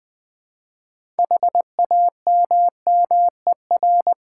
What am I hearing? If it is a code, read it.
Morse code, heard as a keyed tone, about 20 wpm, 700 Hz. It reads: HAMMER
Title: Easy Words with Hints - 20wpm